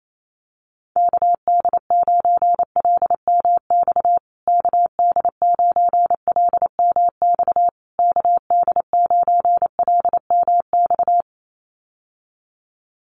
Morse code, heard as a keyed tone, about 28 wpm, 700 Hz. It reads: KD9LMX KD9LMX KD9LMX
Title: USA Call Signs Repeated 3x 28wpm